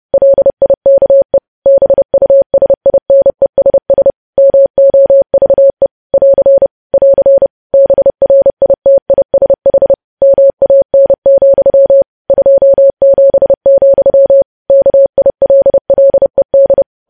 Morse code, heard as a keyed tone, about 30 wpm, 550 Hz. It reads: LIKE BUSINESS MOVE <AR> <AR> BRITISH MAN, 27, KILLED
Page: News Headlines in Morse Code at 30 WPM